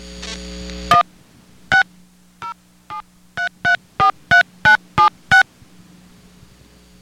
DTMF, heard as telephone touch tones, tones of about 120 ms. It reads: 1 6 073343976
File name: VerizonPrepaid